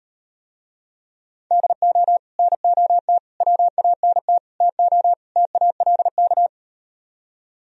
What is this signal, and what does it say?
Morse code, heard as a keyed tone, about 38 wpm, 700 Hz. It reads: DO NOT WANT TO TALK